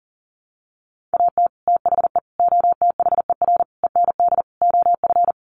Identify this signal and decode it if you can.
Morse code, heard as a keyed tone, about 40 wpm, 700 Hz. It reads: AT THE OTHER END OF
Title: Sets of 5 Words 40wpm